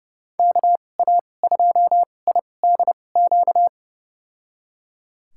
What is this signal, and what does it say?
Morse code, heard as a keyed tone, about 30 wpm, spaced as for 22 wpm, 700 Hz. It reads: KA2IDQ
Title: USA Call Signs - 6 Characters - 2x Character Spacing - 30wpm